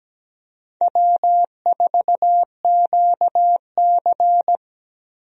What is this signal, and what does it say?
Morse code, heard as a keyed tone, about 17 wpm, 700 Hz. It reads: W4QC
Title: USA Call Signs - 4 Characters - Repeated 2x - 17wpm